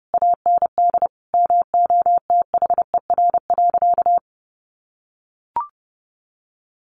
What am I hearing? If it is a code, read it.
Morse code, heard as a keyed tone, about 30 wpm, 700 Hz. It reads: AND MOTHER.
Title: Sentences from Top 1100 Words 30wpm